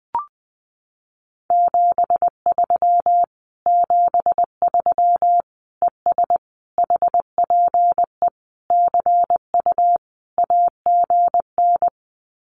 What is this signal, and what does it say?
Morse code, heard as a keyed tone, about 20 wpm, 700 Hz. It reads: 73 73 ES HPE CU AGN
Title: Common QSO Phrases Encore 20wpm